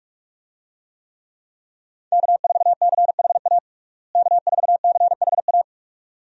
Morse code, heard as a keyed tone, about 46 wpm, 700 Hz. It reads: K4CHA K4CHA
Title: USA Call Signs - 5 Characters - Repeated 2x - 45wpm